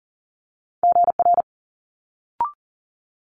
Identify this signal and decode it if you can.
Morse code, heard as a keyed tone, about 38 wpm, 700 Hz. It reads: GR
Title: Multi-letter Phonemes Quick Fire 40wpm